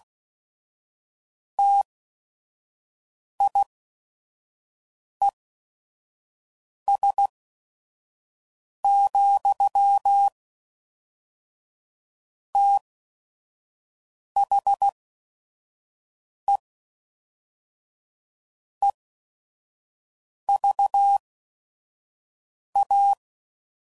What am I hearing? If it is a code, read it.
Morse code, heard as a keyed tone, about 16 wpm, spaced as for 5 wpm, 800 Hz. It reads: TIES, THE EVA
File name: CW Code Groups @5WPM / Technical Subjects / AND Gate